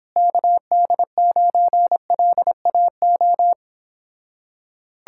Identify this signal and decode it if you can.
Morse code, heard as a keyed tone, about 29 wpm, 700 Hz. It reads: KD9LAO